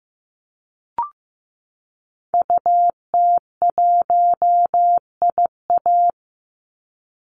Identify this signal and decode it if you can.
Morse code, heard as a keyed tone, about 15 wpm, 700 Hz. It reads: UT1IA
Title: Call Signs Repeated 3x - 15wpm